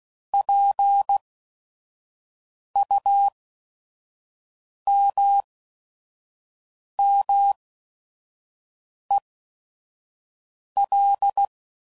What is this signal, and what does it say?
Morse code, heard as a keyed tone, about 16 wpm, spaced as for 5 wpm, 800 Hz. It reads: PUMMEL
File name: Prefixes / By First Letter / Ps